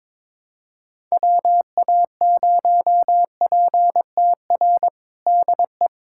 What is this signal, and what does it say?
Morse code, heard as a keyed tone, about 22 wpm, 700 Hz. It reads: WA0PTR DE
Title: USA Callsign DE Callsign - 6 Characters - 22wpm